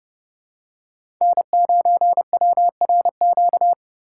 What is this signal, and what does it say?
Morse code, heard as a keyed tone, about 30 wpm, 700 Hz. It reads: N9WRQ
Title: USA Call Signs - 5 Characters - 3x Character Spacing - 30wpm